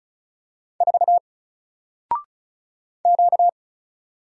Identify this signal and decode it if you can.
Morse code, heard as a keyed tone, about 35 wpm, 700 Hz. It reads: 4 Q
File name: Single-Letter-Number-35wpm